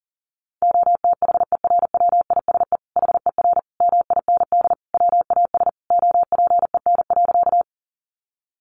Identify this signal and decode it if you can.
Morse code, heard as a keyed tone, about 37 wpm, 700 Hz. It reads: OTHERWISE HER MIND WAS OPEN.
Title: Sentences from Top 1200 Words 40wpm